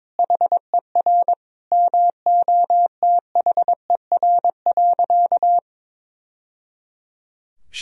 Morse code, heard as a keyed tone, about 22 wpm, 700 Hz. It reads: HER MOTHER.
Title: Sentences from Top 1000 Words - Review Part 4 - 22wpm